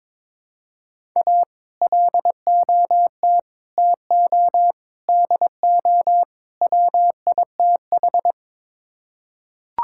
Morse code, heard as a keyed tone, about 22 wpm, 700 Hz. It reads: A LOT TO DO WITH
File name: practice / 22wpm / Build A Phrase - 5-Word Phrases - 22wpm